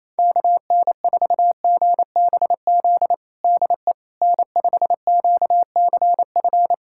Morse code, heard as a keyed tone, about 28 wpm, 700 Hz. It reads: KN4GBZ DE N5QCF
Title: USA Callsign DE Callsign 28wpm